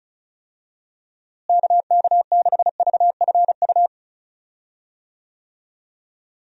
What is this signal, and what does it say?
Morse code, heard as a keyed tone, about 34 wpm, 700 Hz. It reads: KK6VFU
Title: USA Call Signs - 6 Characters - Repeated 3x - 35wpm